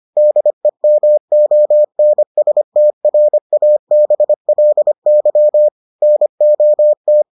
Morse code, heard as a keyed tone, about 25 wpm, 600 Hz. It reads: DEMONSTRABLY NOT